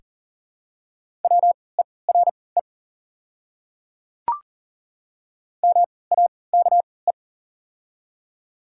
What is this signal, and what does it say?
Morse code, heard as a keyed tone, about 40 wpm, spaced as for 23 wpm, 690 Hz. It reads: WERE MAKE
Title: 3x Character Spacing - Top 100 Words - 40wpm